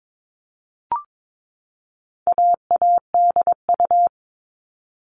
Morse code, heard as a keyed tone, about 22 wpm, 700 Hz. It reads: AADU
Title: All Cut Numbers - 4-Digit 22wpm